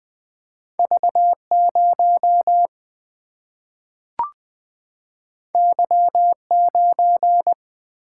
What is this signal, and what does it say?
Morse code, heard as a keyed tone, about 20 wpm, 700 Hz. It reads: V0 Y9
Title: Single-Letter Single-Number 20wpm